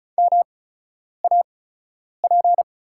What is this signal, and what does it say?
Morse code, heard as a keyed tone, about 36 wpm, spaced as for 10 wpm, 700 Hz. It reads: MAP
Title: 8x Character Spacing - Top 500 Words - 35wpm